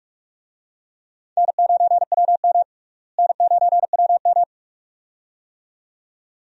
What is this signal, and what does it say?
Morse code, heard as a keyed tone, about 43 wpm, 700 Hz. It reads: N9WM N9WM
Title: USA Call Signs - 4 Characters - Repeated 2x - 45wpm